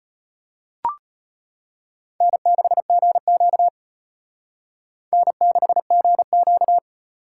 Morse code, heard as a keyed tone, about 29 wpm, 700 Hz. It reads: N6GQ N6GQ
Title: Speed Racing - USA 1x2 Call Signs 25wpm